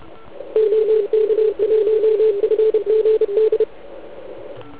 Morse code, heard as a keyed tone, about 33 wpm, 410 Hz. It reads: OK1FGD
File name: Archiv / CUC149 / OK1IF CUC149